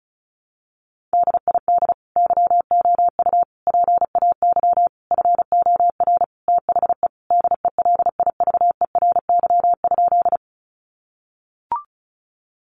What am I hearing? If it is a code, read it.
Morse code, heard as a keyed tone, about 34 wpm, 700 Hz. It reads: DID YOU PAY FOR THE DELIVERY?
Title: Sentences from Top 1500 Words 35wpm